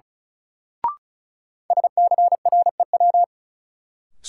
Morse code, heard as a keyed tone, about 34 wpm, 700 Hz. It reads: SCREW
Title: Sentences from Top 1600 Words - New Word Drill Alt - 35wpm